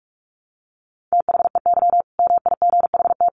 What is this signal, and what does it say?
Morse code, heard as a keyed tone, about 44 wpm, 700 Hz. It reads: THEY MIGHT
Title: Sentences from Top 1000 Words 45wpm